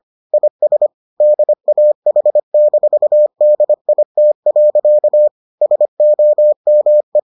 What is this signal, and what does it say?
Morse code, heard as a keyed tone, about 25 wpm, 600 Hz. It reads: IS DAH-DIT. SOME